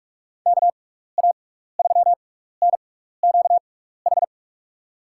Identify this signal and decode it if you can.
Morse code, heard as a keyed tone, about 46 wpm, spaced as for 15 wpm, 700 Hz. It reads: KA3NQH
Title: USA Call Signs - 6 Characters - 6x Character Spacing - 45wpm